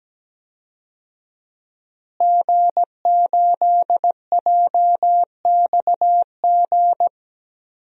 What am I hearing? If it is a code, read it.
Morse code, heard as a keyed tone, about 17 wpm, 700 Hz. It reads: G8JXG
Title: DXCC-100 Call Signs - 5 Characters - Repeated 2x - 17wpm